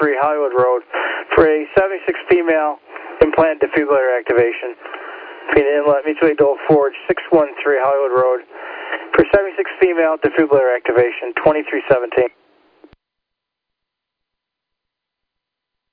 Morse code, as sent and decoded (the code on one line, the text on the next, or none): none